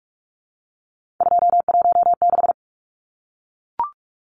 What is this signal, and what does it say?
Morse code, heard as a keyed tone, about 44 wpm, 700 Hz. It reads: 216
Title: Numbers - 3-Digit 45wpm